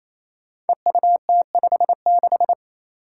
Morse code, heard as a keyed tone, about 28 wpm, 700 Hz. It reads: EUT56
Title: Code Groups - 5 Letter:Numbers - 28wpm